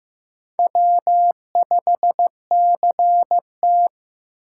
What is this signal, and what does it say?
Morse code, heard as a keyed tone, about 15 wpm, 700 Hz. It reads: W5CT
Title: USA Call Signs - 4 Characters - 8x Character Spacing - 15wpm